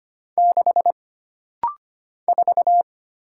Morse code, heard as a keyed tone, about 25 wpm, 700 Hz. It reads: 6 4
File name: Single Character (Letter:Number:Punctuation:Prosign) - Rapid-Fire - 25wpm